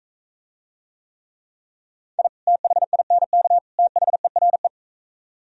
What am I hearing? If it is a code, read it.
Morse code, heard as a keyed tone, about 43 wpm, 700 Hz. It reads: I THINK THERE